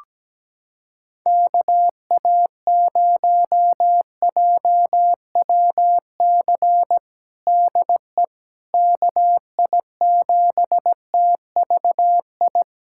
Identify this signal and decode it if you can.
Morse code, heard as a keyed tone, about 17 wpm, 700 Hz. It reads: KA0JWC DE KI7TVI
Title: USA Callsign DE Callsign - 6 Characters - 17wpm